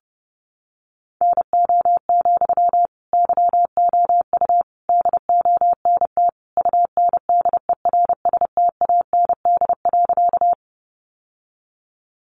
Morse code, heard as a keyed tone, about 30 wpm, 700 Hz. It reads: NO, YOU DONT UNDERSTAND.
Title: Sentences from Top 600 Words 30wpm